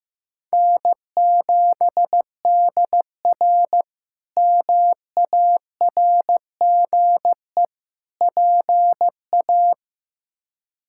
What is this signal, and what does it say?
Morse code, heard as a keyed tone, about 15 wpm, 700 Hz. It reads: N7DR MARGE PA
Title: K1USN SST QSO Practice - 15wpm